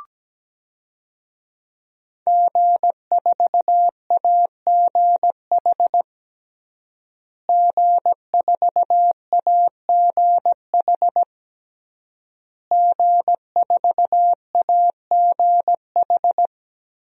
Morse code, heard as a keyed tone, about 17 wpm, 700 Hz. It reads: G4AGH G4AGH G4AGH
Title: DXCC-100 Call Signs - 5 Characters - Repeated 3x - 17wpm